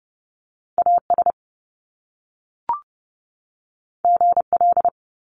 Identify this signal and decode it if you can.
Morse code, heard as a keyed tone, about 30 wpm, 700 Hz. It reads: AS GL
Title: Top Two Letter Combinations in Words 30wpm